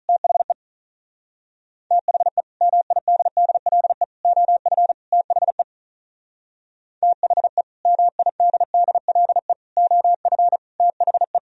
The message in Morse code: THE THE MIDDLE OF THE THE MIDDLE OF THE

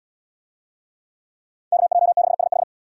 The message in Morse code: KJ6SH

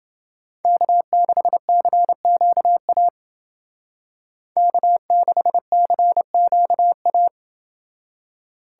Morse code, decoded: K6CQA K6CQA

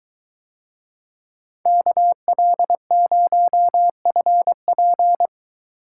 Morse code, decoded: KL0FP

